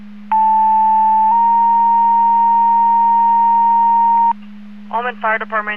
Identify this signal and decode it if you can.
Two-tone page, heard as a one-second tone, then a three-second tone, about 880 then 930 Hz, with speech. It is heard from 0.3 to 4.3 s.